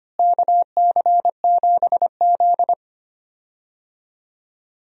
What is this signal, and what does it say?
Morse code, heard as a keyed tone, about 25 wpm, 700 Hz. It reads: KC7Z
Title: USA Call Signs - 4 Characters - Repeated 2x - 25wpm